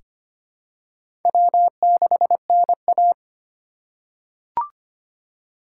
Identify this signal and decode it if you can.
Morse code, heard as a keyed tone, about 25 wpm, 700 Hz. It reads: W6NA
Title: USA Call Signs - 4 Characters - 3x Character Spacing - 25wpm